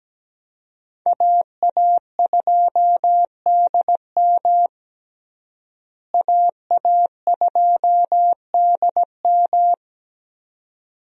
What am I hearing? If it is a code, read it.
Morse code, heard as a keyed tone, about 17 wpm, 700 Hz. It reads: AA2DM AA2DM
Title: USA Call Signs - 5 Characters - Repeated 2x - 17wpm